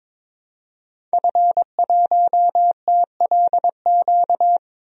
Morse code, heard as a keyed tone, about 22 wpm, 700 Hz. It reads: F1TLQ
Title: DXCC-100 Call Signs - 5 Characters - 22wpm